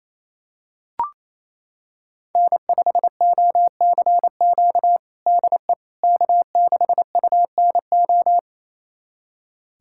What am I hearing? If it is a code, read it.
Morse code, heard as a keyed tone, about 28 wpm, 700 Hz. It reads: N5OCQ DE K6UNO
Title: USA Callsign DE Callsign - 5 Characters - 28wpm